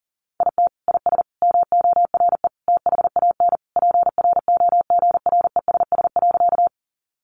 Morse code, heard as a keyed tone, about 40 wpm, 700 Hz. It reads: IT IS MORE THAN PROGRESS.
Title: Sentences from Top 1400 Words 40wpm